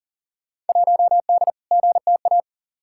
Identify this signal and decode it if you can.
Morse code, heard as a keyed tone, about 40 wpm, 700 Hz. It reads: 1D GTA
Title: Field Day Exchanges - 40wpm